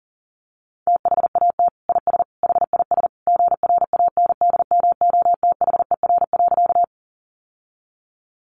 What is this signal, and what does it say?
Morse code, heard as a keyed tone, about 40 wpm, 700 Hz. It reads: THAT IS HIS GRANDMOTHER.
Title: Sentences from Top 1200 Words 40wpm